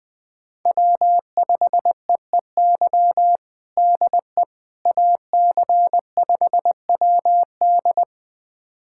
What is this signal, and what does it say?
Morse code, heard as a keyed tone, about 20 wpm, 700 Hz. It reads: W5EEY DE AC5WD
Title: USA Callsign DE Callsign - 5 Characters - 20wpm